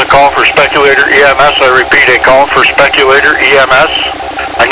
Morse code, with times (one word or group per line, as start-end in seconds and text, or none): none